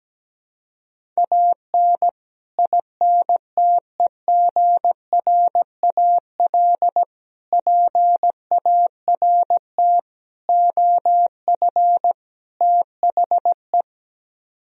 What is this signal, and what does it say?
Morse code, heard as a keyed tone, about 17 wpm, 700 Hz. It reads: AN INTEGRAL PART OF THE
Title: Speed-Racing - Sets of 5 Words 17wpm